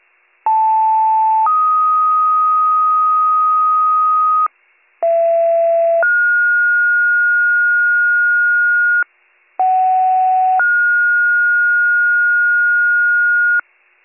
Two-tone page, heard as a one-second tone, then a three-second tone, about 880 then 1250 Hz, about 670 then 1430 Hz, about 750 then 1430 Hz, with noise. It is heard from 0.5 to 13.6 s.